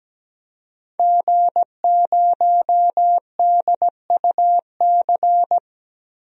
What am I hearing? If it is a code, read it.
Morse code, heard as a keyed tone, about 17 wpm, 700 Hz. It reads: G0DUC
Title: DXCC-100 Call Signs - 5 Characters - 17wpm